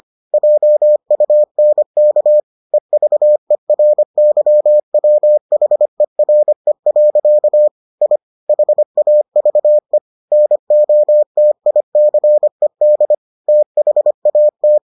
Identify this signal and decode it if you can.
Morse code, heard as a keyed tone, about 25 wpm, 600 Hz. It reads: JUNK EVERYWHERE. I HAVE NOTICED THAT